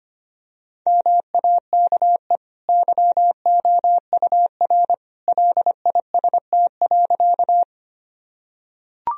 Morse code, heard as a keyed tone, about 25 wpm, 700 Hz. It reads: MAKE YOUR LIST.
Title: Sentences from Top 1000 Words - Review Part 2 - 25wpm